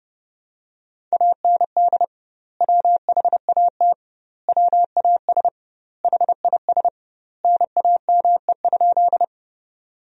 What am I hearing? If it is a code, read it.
Morse code, heard as a keyed tone, about 32 wpm, 700 Hz. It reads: AND WHAT WAS HIS NAME?
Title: 2x Word-Spacing - Sentences from Top 200 Words - 30wpm